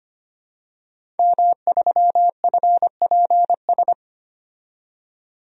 Morse code, heard as a keyed tone, about 25 wpm, 700 Hz. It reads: M3FPS
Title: DXCC-100 Call Signs - 5 Characters - 25wpm